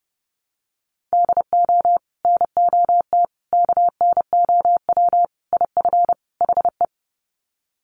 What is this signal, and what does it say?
Morse code, heard as a keyed tone, about 30 wpm, 700 Hz. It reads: DO NOT KNOW IF HE
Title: Sets of 5 Words Encore2 30wpm